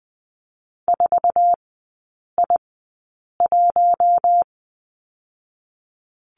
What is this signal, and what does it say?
Morse code, heard as a keyed tone, about 20 wpm, spaced as for 8 wpm, 700 Hz. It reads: 4I1
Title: 047 Lesson 9e - Number 4 - Three Characters 20wpm